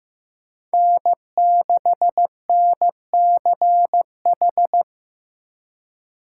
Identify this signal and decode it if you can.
Morse code, heard as a keyed tone, about 15 wpm, 700 Hz. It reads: N6NCH
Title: USA Call Signs - 5 Characters - 8x Character Spacing - 15wpm